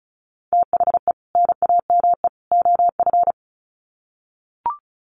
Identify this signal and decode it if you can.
Morse code, heard as a keyed tone, about 34 wpm, 700 Hz. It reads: THE NAME OF
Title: Sets of 3 Words 35wpm